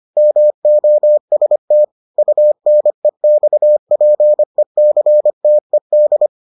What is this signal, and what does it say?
Morse code, heard as a keyed tone, about 25 wpm, 600 Hz. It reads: MOST UNEXPECTED